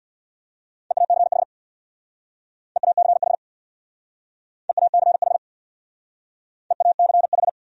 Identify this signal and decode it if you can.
Morse code, heard as a keyed tone, about 65 wpm, 700 Hz. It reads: EACH EACH EACH EACH